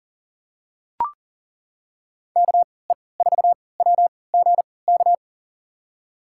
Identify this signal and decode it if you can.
Morse code, heard as a keyed tone, about 40 wpm, spaced as for 23 wpm, 700 Hz. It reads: KE4WGK
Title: USA Call Signs - 3x Character Spacing - 40wpm